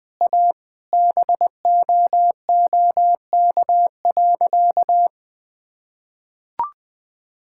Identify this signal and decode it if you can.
Morse code, heard as a keyed tone, about 20 wpm, 700 Hz. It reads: A BOOK.
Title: Sentences with 5-Word Phrases - 20wpm